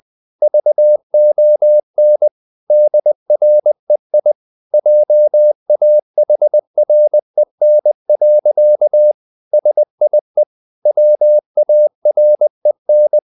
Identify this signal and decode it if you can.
Morse code, heard as a keyed tone, about 20 wpm, 600 Hz. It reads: VON DREI JAHREN. SIE WAREN